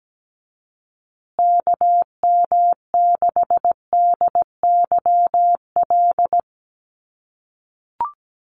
Morse code, heard as a keyed tone, about 17 wpm, 700 Hz. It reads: KM6DYL
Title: USA Call Signs Repeated 3x 17wpm